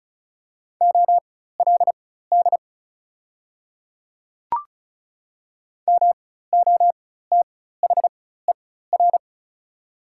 Morse code, heard as a keyed tone, about 34 wpm, spaced as for 16 wpm, 700 Hz. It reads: OLD MOTHER
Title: 4x Character Spacing - Top 200 Words - 35wpm